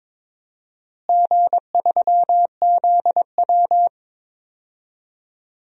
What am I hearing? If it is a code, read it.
Morse code, heard as a keyed tone, about 22 wpm, 700 Hz. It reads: G3ZW
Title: DXCC-100 Call Signs - 4 Characters - Repeated 3x - 22wpm